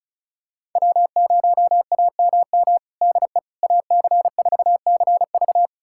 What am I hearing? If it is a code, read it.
Morse code, heard as a keyed tone, about 34 wpm, 700 Hz. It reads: W0AMM DE AC4CV
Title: USA Callsign DE Callsign 35wpm